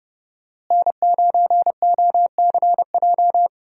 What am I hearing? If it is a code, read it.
Morse code, heard as a keyed tone, about 30 wpm, 700 Hz. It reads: N9OCJ
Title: USA Call Signs - 5 Characters - Repeated 3x - 30wpm